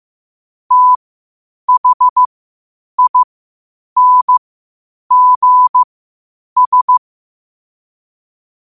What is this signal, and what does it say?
Morse code, heard as a keyed tone, about 15 wpm, spaced as for 8 wpm, 1000 Hz. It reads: THINGS
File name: qotdfiles / 8wpm